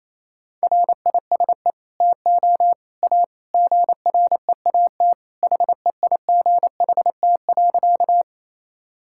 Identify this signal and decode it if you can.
Morse code, heard as a keyed tone, about 28 wpm, 700 Hz. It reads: RISE TO A GREAT HEIGHT.
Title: Sentences from Top 1600 Words 28wpm